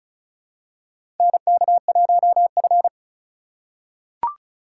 Morse code, heard as a keyed tone, about 36 wpm, 700 Hz. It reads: NK1F
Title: USA Call Signs - 4 Characters - 3x Character Spacing - 35wpm